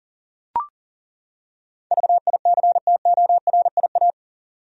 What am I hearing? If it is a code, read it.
Morse code, heard as a keyed tone, about 40 wpm, 700 Hz. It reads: VICTORIA
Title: USA Common Names - 40wpm